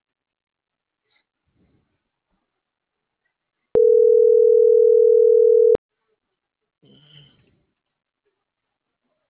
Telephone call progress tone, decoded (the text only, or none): ringback tone